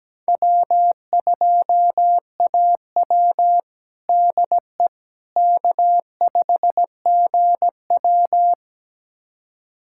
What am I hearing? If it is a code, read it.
Morse code, heard as a keyed tone, about 17 wpm, 700 Hz. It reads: W2AW DE K5GW
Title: USA Callsign DE Callsign - 4 Characters - 17wpm